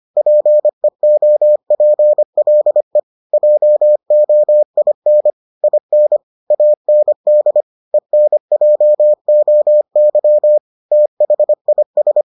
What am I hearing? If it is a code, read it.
Morse code, heard as a keyed tone, about 25 wpm, 600 Hz. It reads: PEOPLE JOIN IN AND ENJOY THIS